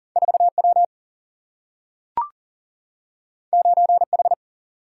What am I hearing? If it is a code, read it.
Morse code, heard as a keyed tone, about 38 wpm, 700 Hz. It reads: 4 W 9 H
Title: Single-Letter Single-Number 40wpm